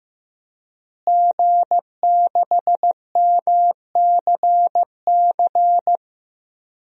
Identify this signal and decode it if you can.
Morse code, heard as a keyed tone, about 16 wpm, 700 Hz. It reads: G6MCC
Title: DXCC-100 Call Signs - 5 Characters - Repeated 2x - 15wpm